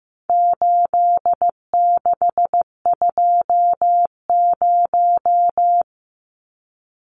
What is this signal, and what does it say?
Morse code, heard as a keyed tone, about 15 wpm, 700 Hz. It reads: 8620
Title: Numbers - 4-Digit 15wpm